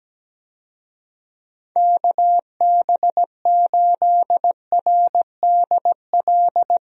Morse code, heard as a keyed tone, about 17 wpm, 700 Hz. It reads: KB8RDL
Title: USA Call Signs - 6 Characters - Repeated 2x - 17wpm